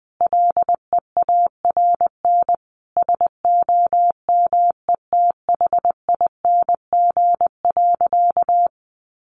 Morse code, heard as a keyed tone, about 20 wpm, 700 Hz. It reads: LEARN SOMETHING.
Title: Sentences from Top 600 Words 20wpm